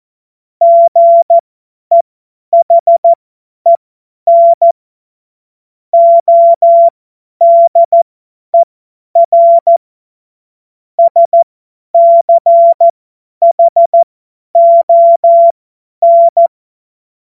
Morse code, heard as a keyed tone, about 14 wpm, spaced as for 10 wpm, 680 Hz. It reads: GEHEN ODER SCHON
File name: Witz-21-Oktober-2025-50-F70